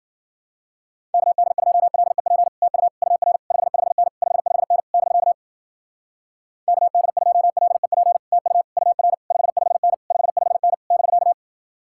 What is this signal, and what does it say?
Morse code, heard as a keyed tone, about 49 wpm, 700 Hz. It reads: KD2LEP TU UR 55N 55N <BK> KD2LEP TU UR 55N 55N <BK>
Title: Speed-Racing Common QSO Phrases Encore - 40wpm